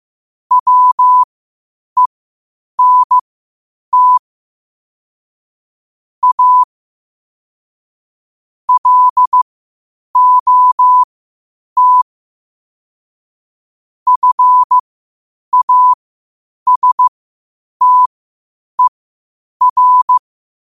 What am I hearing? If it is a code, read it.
Morse code, heard as a keyed tone, about 15 wpm, spaced as for 8 wpm, 1000 Hz. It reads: WENT A LOT FASTER
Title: 8 WPM morse code quotes for Thu, 29 Feb 2024